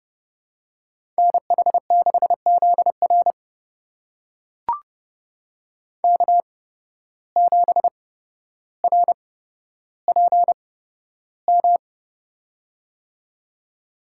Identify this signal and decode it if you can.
Morse code, heard as a keyed tone, about 30 wpm, spaced as for 13 wpm, 700 Hz. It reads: NH6ZR K7RPM
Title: USA Call Signs - 8x Character Spacing - 30wpm